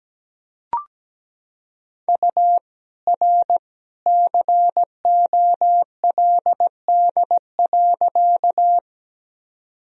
Morse code, heard as a keyed tone, about 17 wpm, 700 Hz. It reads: U R COLD.
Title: Sentences from Top 400 Words and Abbreviations 17wpm